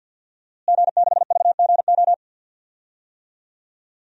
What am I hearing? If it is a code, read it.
Morse code, heard as a keyed tone, about 50 wpm, 700 Hz. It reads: G6VGO
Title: DXCC-100 Call Signs - 5 Characters - Repeated 2x - 50wpm